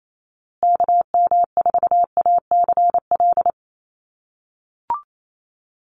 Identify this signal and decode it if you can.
Morse code, heard as a keyed tone, about 28 wpm, 700 Hz. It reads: KM4ACL
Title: USA Call Signs Repeated 3x 28wpm